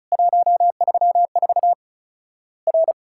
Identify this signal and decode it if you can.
Morse code, heard as a keyed tone, about 36 wpm, 700 Hz. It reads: 134 R
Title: Worked All Europe DX Contest - QTC Exchange - 35wpm